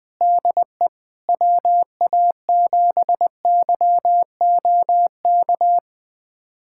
Morse code, heard as a keyed tone, about 21 wpm, 700 Hz. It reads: DE WA7YOK
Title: USA Callsign DE Callsign - 6 Characters - 20wpm